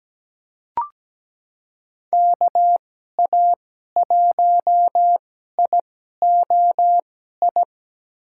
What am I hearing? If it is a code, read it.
Morse code, heard as a keyed tone, about 17 wpm, spaced as for 12 wpm, 700 Hz. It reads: KA1IOI
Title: USA Call Signs - 6 Characters - 2x Character Spacing - 17wpm